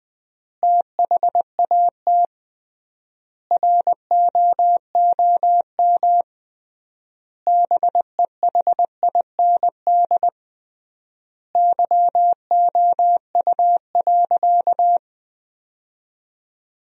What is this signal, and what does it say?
Morse code, heard as a keyed tone, about 20 wpm, 700 Hz. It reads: THAT ROOM BEHIND YOU.
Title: 3x Word-Spacing - Sentences from Top 500 Words - 20wpm